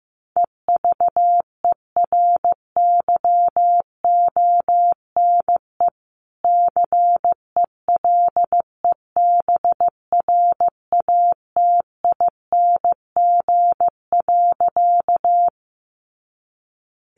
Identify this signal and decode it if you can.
Morse code, heard as a keyed tone, about 15 wpm, 700 Hz. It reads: EVERYONE CELEBRATING.
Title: A Sound Of Thunder 15wpm